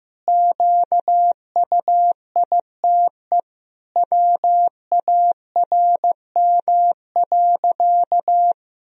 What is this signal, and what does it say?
Morse code, heard as a keyed tone, about 15 wpm, 700 Hz. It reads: QUITE WARM.
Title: Sentences from Top 1000 Words - Review Part 2 - 15wpm